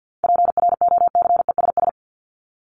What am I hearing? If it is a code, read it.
Morse code, heard as a keyed tone, about 53 wpm, 700 Hz. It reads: PROCESS
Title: ING Suffix - Top 100 Words with Base Words 50wpm